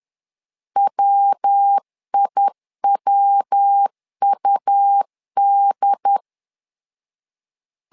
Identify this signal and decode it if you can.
Morse code, heard as a keyed tone, about 10 wpm, 800 Hz. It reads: WIWUD
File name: LEZIONI / 3 MWRO / 3 MW 10wpm 1959